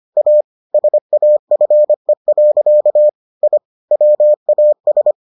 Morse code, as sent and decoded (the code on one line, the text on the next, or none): .- / ... .- ..-. .-.-.- / .. / .-- .- ...
A SAFE. I WAS